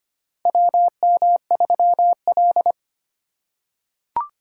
Morse code, heard as a keyed tone, about 25 wpm, 700 Hz. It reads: WM3L
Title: USA Call Signs - 4 Characters - 3x Character Spacing - 25wpm